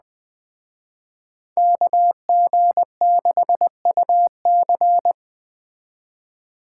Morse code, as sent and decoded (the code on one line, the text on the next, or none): -.- --. -.... ..- -.-.
KG6UC